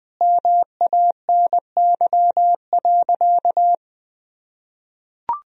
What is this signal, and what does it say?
Morse code, heard as a keyed tone, about 20 wpm, 700 Hz. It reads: MANY.
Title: Sentences from Top 1000 Words - Review Part 1 - 20wpm